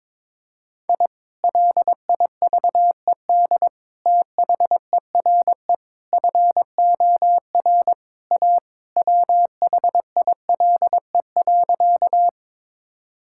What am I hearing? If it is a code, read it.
Morse code, heard as a keyed tone, about 22 wpm, 700 Hz. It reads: I LIVED THERE FOR A WHILE.
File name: practice / 22wpm / Sentences with 3-Word Phrases - 22wpm